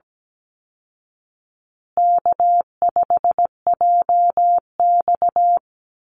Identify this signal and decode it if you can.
Morse code, heard as a keyed tone, about 17 wpm, 700 Hz. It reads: K5JX
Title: Call Signs Repeated 2x - 17wpm